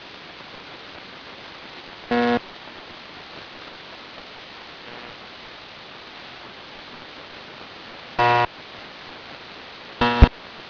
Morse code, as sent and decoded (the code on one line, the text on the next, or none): none